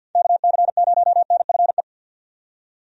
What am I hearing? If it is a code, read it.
Morse code, heard as a keyed tone, about 50 wpm, 700 Hz. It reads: KC0NFE
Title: USA Call Signs - 6 Characters - 50wpm